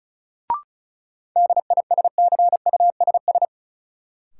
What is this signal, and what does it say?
Morse code, heard as a keyed tone, about 36 wpm, 700 Hz. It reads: DISCUSS
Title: Sentences from Top 1000 Words - New Word Drill - 35wpm